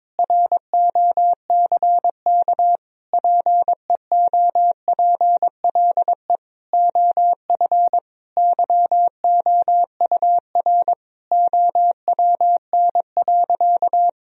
Morse code, heard as a keyed tone, about 21 wpm, 700 Hz. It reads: ROCK PEOPLE OF YOUR OWN.
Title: Sentences from Top 1000 Words - Review Part 2 - 22wpm